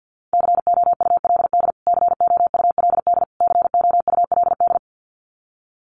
Morse code, heard as a keyed tone, about 50 wpm, 700 Hz. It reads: COULD COULD COULD
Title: Top 200 Words Repeated 50wpm